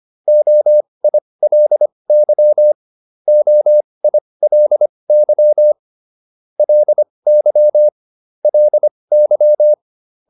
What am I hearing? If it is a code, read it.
Morse code, heard as a keyed tone, about 25 wpm, spaced as for 20 wpm, 600 Hz. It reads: OILY OILY LY LY